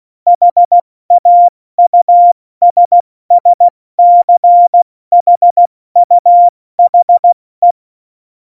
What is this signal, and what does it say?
Morse code, heard as a keyed tone, about 16 wpm, 700 Hz. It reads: HAUSSCHUHE
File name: Morsekurs / Witze / Witz 12. Dezember 2017, 70 F80